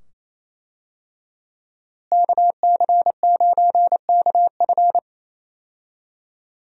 Morse code, heard as a keyed tone, about 28 wpm, 700 Hz. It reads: KC9KF
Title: USA Call Signs - 5 Characters - Repeated 2x - 28wpm